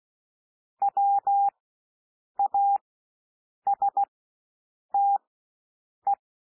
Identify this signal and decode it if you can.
Morse code, heard as a keyed tone, about 16 wpm, spaced as for 7 wpm, 800 Hz. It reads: WASTE
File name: cwmp3 / W1 SOUND WORDS 16 wpm